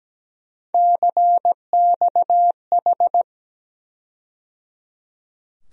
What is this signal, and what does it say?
Morse code, heard as a keyed tone, about 17 wpm, 700 Hz. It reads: CXH
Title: USA Call Sign Postfixes - 3 Letters 17wpm